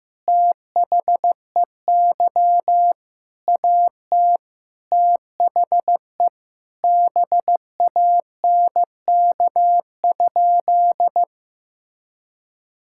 Morse code, heard as a keyed tone, about 15 wpm, 700 Hz. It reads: THEY AT THE BANK?
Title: Sentences from Top 1000 Words - Review Part 6 - 15wpm